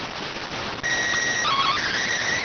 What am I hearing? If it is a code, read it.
SSTV (slow-scan television), PD 290.